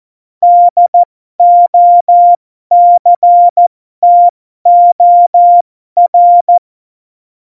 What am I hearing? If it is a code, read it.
Morse code, heard as a keyed tone, about 14 wpm, spaced as for 12 wpm, 700 Hz. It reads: DOCTOR
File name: Morsekurs / Witze